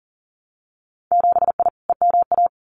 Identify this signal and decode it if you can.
Morse code, heard as a keyed tone, about 42 wpm, 700 Hz. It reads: 7I EMA
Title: Winter Field Day Exchanges Encore 40wpm